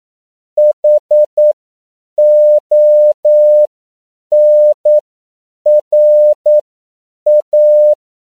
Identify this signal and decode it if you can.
Morse code, heard as a keyed tone, about 9 wpm, spaced as for 7 wpm, 600 Hz. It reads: HONRA
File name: CW / 10PPM